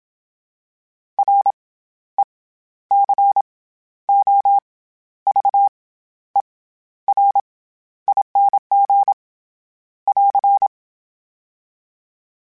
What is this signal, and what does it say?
Morse code, heard as a keyed tone, about 27 wpm, 800 Hz. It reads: R E C O V E R ING <AR>
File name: Special INGs-mp3s / Special INGs-mp3s / SlowToFaster ing @25